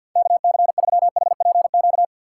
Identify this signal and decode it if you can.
Morse code, heard as a keyed tone, about 50 wpm, 700 Hz. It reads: KC3HPQ
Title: USA Call Signs - 6 Characters - 4x Character Spacing - 50wpm